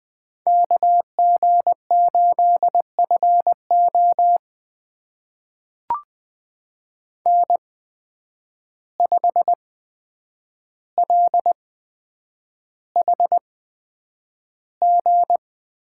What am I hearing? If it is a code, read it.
Morse code, heard as a keyed tone, about 20 wpm, spaced as for 9 wpm, 700 Hz. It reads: KG8FO N5LHG
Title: USA Call Signs - 5 Characters - 8x Character Spacing - 20wpm